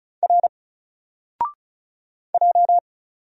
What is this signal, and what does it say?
Morse code, heard as a keyed tone, about 37 wpm, 700 Hz. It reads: R J
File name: Single-Letter-Number-35wpm